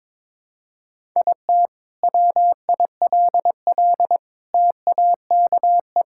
Morse code, heard as a keyed tone, about 22 wpm, 700 Hz. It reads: IT WILL TAKE
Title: Sentences from Top 1000 Words - Review Part 5 - 22wpm